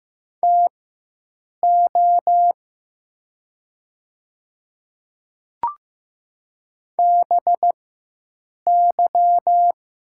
Morse code, heard as a keyed tone, about 15 wpm, spaced as for 7 wpm, 700 Hz. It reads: TO BY
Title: 4x Character Spacing - Top 100 Words - 15wpm